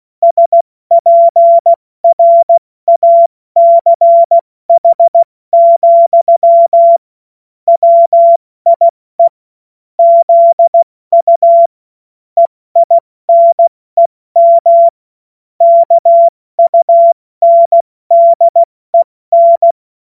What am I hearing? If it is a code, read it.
Morse code, heard as a keyed tone, about 16 wpm, 680 Hz. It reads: SPRACH, WIE ZU EINEM KUNDEN
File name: Morsekurs / Witze / Witz 10. Mai 2022 70 F80